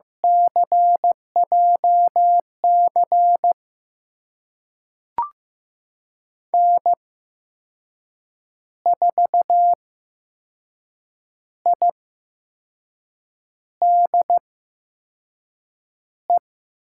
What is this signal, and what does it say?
Morse code, heard as a keyed tone, about 15 wpm, spaced as for 5 wpm, 700 Hz. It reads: CJC N4IDE
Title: USA Call Signs - 8x Character Spacing - 15wpm